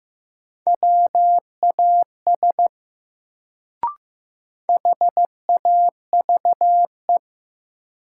Morse code, heard as a keyed tone, about 15 wpm, 700 Hz. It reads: WAS HAVE
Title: Sentences from Top 100 Words - New Word Drill - 15wpm